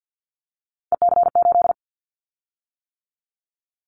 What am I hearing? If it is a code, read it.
Morse code, heard as a keyed tone, about 50 wpm, 700 Hz. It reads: EC8
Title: Callsign Prefixes for Top 50 DXCC - 50wpm